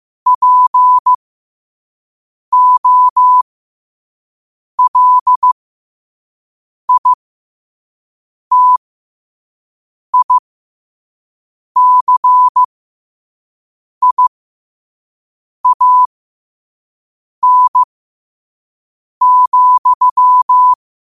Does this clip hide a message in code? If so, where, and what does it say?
Yes: Morse code from 0.3 to 20.7 s, heard as a keyed tone, about 15 wpm, spaced as for 5 wpm, 1000 Hz. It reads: POLITICIAN,